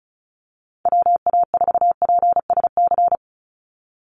Morse code, heard as a keyed tone, about 34 wpm, 700 Hz. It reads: WA4PSC
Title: Call Signs - 35wpm